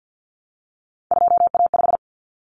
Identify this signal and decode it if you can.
Morse code, heard as a keyed tone, about 50 wpm, 700 Hz. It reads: 2A5
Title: Common Cut Numbers - 3-Digit 50wpm